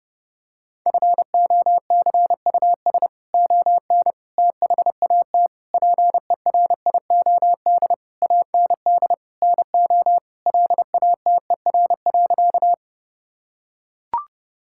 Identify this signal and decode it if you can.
Morse code, heard as a keyed tone, about 30 wpm, 700 Hz. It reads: FOCUS ON THAT PERIOD AND NO LATER.
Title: Sentences from Top 1000 Words - Review Part 5 - 30wpm